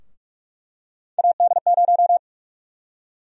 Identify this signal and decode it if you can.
Morse code, heard as a keyed tone, about 46 wpm, 700 Hz. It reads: AD0
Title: USA Call Sign Prefixes - Repeated 2x - 45wpm